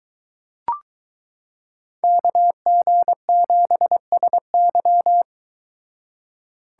Morse code, heard as a keyed tone, about 23 wpm, 700 Hz. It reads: KG7SY